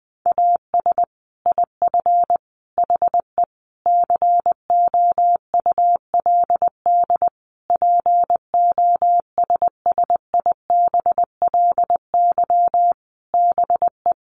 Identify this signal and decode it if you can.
Morse code, heard as a keyed tone, about 20 wpm, 700 Hz. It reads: AS IF HE COULD POSSIBLY BE
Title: Sentences from Top 1300 Words 20wpm